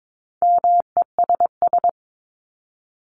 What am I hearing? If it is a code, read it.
Morse code, heard as a keyed tone, about 22 wpm, 700 Hz. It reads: MESS
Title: Top 1751-2000 Words 22wpm